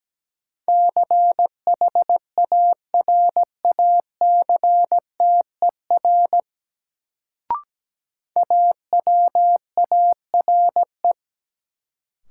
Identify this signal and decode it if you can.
Morse code, heard as a keyed tone, about 17 wpm, 700 Hz. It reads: CHARACTER AWARE
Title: Sentences from Top 1200 Words - New Word Drill - 17wpm